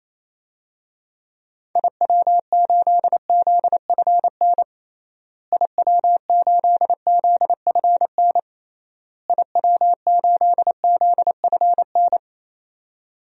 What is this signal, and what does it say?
Morse code, heard as a keyed tone, about 27 wpm, 700 Hz. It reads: IW8ZFN IW8ZFN IW8ZFN
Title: DXCC-100 Call Signs - 6 Characters - Repeated 3x - 28wpm